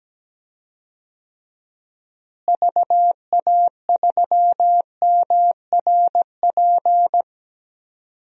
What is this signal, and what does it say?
Morse code, heard as a keyed tone, about 17 wpm, 700 Hz. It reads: VA3MRP
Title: DXCC-100 Call Signs - 6 Characters - Repeated 2x - 17wpm